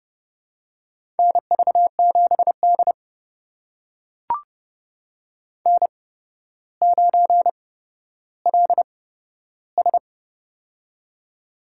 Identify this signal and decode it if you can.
Morse code, heard as a keyed tone, about 28 wpm, spaced as for 13 wpm, 700 Hz. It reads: NV7D N9LS
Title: USA Call Signs - 4 Characters - 8x Character Spacing - 30wpm